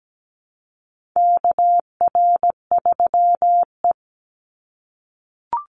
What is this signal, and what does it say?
Morse code, heard as a keyed tone, about 17 wpm, 700 Hz. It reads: KR3E
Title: Call Signs - 17wpm